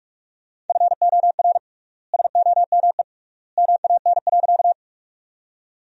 Morse code, heard as a keyed tone, about 46 wpm, 700 Hz. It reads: FOR SOME MAN.
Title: 3x Word-Spacing - Sentences from Top 200 Words - 45wpm